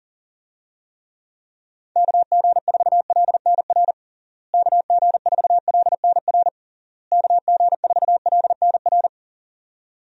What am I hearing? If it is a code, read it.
Morse code, heard as a keyed tone, about 40 wpm, 700 Hz. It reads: KG4LNR KG4LNR KG4LNR
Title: USA Call Signs - 6 Characters - Repeated 3x - 40wpm